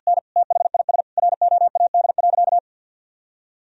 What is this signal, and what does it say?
Morse code, heard as a keyed tone, about 50 wpm, 700 Hz. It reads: N THIS ROAD.
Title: Sentences from Top 400 Words and Abbreviations 50wpm